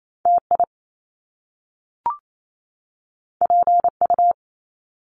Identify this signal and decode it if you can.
Morse code, heard as a keyed tone, about 28 wpm, 700 Hz. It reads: TI PU
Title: Top Two Letter Combinations in Words 28wpm